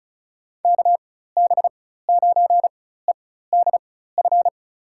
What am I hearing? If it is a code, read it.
Morse code, heard as a keyed tone, about 34 wpm, spaced as for 16 wpm, 700 Hz. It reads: KB9EDF
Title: USA Call Signs - 4x Character Spacing - 35wpm